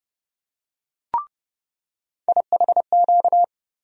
Morse code, heard as a keyed tone, about 30 wpm, 700 Hz. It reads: IHQ
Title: USA Call Sign Postfixes - 3 Letters 30wpm